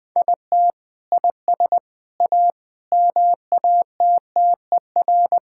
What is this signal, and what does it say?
Morse code, heard as a keyed tone, about 20 wpm, 700 Hz. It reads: IT IS A MATTER
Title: Sentences with 3-Word Phrases - Spoken Beginning - 20wpm